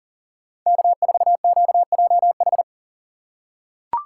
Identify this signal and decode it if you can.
Morse code, heard as a keyed tone, about 40 wpm, 700 Hz. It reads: K4QJH
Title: USA Call Signs - 5 Characters - 3x Character Spacing - 40wpm